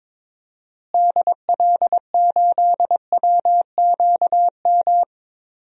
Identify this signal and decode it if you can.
Morse code, heard as a keyed tone, about 22 wpm, 700 Hz. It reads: DL8WQM
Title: DXCC-100 Call Signs - 6 Characters - Repeated 2x - 22wpm